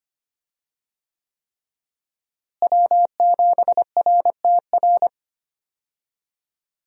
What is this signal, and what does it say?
Morse code, heard as a keyed tone, about 25 wpm, 700 Hz. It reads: W7RTR
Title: USA Call Signs - 5 Characters - Repeated 3x - 25wpm